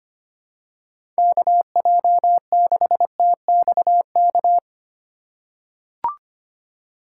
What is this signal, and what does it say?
Morse code, heard as a keyed tone, about 25 wpm, 700 Hz. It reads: KJ6TXK
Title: USA Call Signs - 6 Characters - 4x Character Spacing - 25wpm